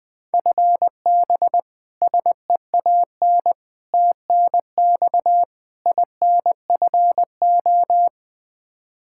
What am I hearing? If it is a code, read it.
Morse code, heard as a keyed tone, about 21 wpm, 700 Hz. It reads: FB SEAN TNX INFO
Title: Practice QSO - Exchange 3 - More Info - 20wpm